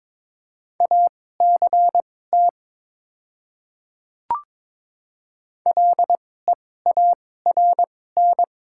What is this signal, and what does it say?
Morse code, heard as a keyed tone, about 22 wpm, spaced as for 16 wpm, 700 Hz. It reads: ACT LEARN
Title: 2x Character Spacing - Top 500 Words - 22wpm